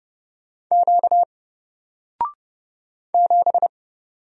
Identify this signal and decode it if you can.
Morse code, heard as a keyed tone, about 30 wpm, 700 Hz. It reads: Q 7
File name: Single-Letter-Number-30wpm